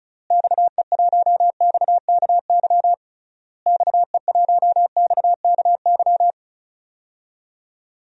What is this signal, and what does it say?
Morse code, heard as a keyed tone, about 36 wpm, 700 Hz. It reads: XE1XKY XE1XKY
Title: DXCC-100 Call Signs - 6 Characters - Repeated 3x - 35wpm